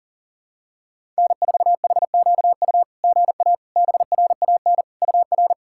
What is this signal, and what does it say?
Morse code, heard as a keyed tone, about 37 wpm, 690 Hz. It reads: N4HQU GA BRAN UR
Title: SOTA QSO Practice 40wpm